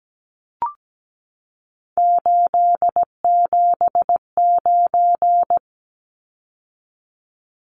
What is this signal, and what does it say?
Morse code, heard as a keyed tone, about 17 wpm, 700 Hz. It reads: 879
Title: Numbers - 3-Digit 17wpm